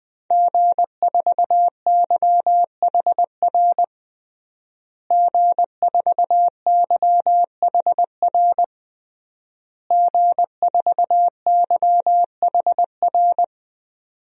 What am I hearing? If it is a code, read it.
Morse code, heard as a keyed tone, about 20 wpm, 700 Hz. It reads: G4YHR G4YHR G4YHR
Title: DXCC-100 Call Signs - 5 Characters - Repeated 3x - 20wpm